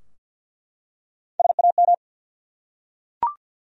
Morse code, heard as a keyed tone, about 49 wpm, 700 Hz. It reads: SAM
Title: USA Common Names - Max 3 Letters - 50wpm